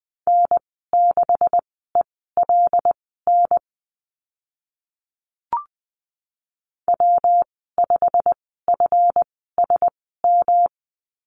Morse code, heard as a keyed tone, about 20 wpm, spaced as for 14 wpm, 700 Hz. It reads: N6ELN W5FSM
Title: 140 Lesson 23g - Letter f - Call Signs 20wpm